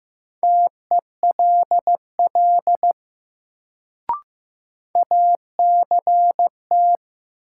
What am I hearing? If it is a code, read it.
Morse code, heard as a keyed tone, about 15 wpm, 700 Hz. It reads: TELL ACT
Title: Sentences from Top 200 Words - New Word Drill - 15wpm